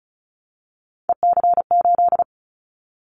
Morse code, heard as a keyed tone, about 34 wpm, 700 Hz. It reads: EC8
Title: Callsign Prefixes for Top 100 DXCC - 35wpm